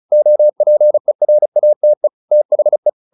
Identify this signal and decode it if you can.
Morse code, heard as a keyed tone, about 34 wpm, 600 Hz. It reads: OPERATE THE